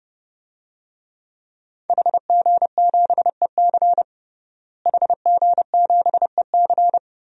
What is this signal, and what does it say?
Morse code, heard as a keyed tone, about 30 wpm, 700 Hz. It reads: HG7EC HG7EC
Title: DXCC-100 Call Signs - 5 Characters - Repeated 3x - 30wpm